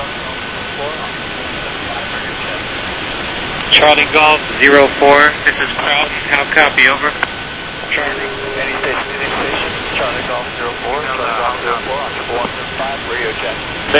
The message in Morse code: none